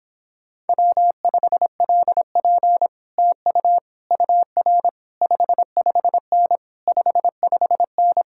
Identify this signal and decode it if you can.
Morse code, heard as a keyed tone, about 31 wpm, 700 Hz. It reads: W5LP TU UR 55N 55N